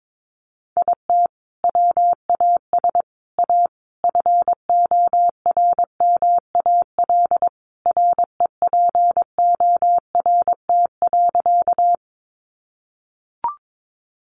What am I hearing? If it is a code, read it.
Morse code, heard as a keyed tone, about 22 wpm, 700 Hz. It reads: IT WAS A FORMAL REPORT.
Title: Sentences from Top 1300 Words 22wpm